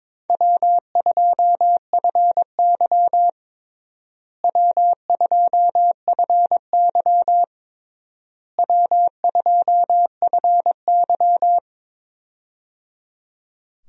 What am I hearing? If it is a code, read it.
Morse code, heard as a keyed tone, about 22 wpm, 700 Hz. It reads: W2FY W2FY W2FY
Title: USA Call Signs - 4 Characters - Repeated 3x - 22wpm